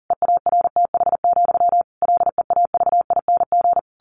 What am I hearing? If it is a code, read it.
Morse code, heard as a keyed tone, about 40 wpm, 700 Hz. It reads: EARTH, LEAVING
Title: A Sound Of Thunder 40wpm